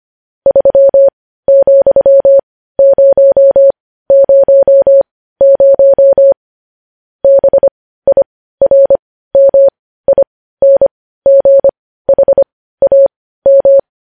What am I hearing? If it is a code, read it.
Morse code, heard as a keyed tone, about 25 wpm, spaced as for 15 wpm, 550 Hz. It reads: 3,000 BIRMINGHAM
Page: News headlines in Morse code at 15 wpm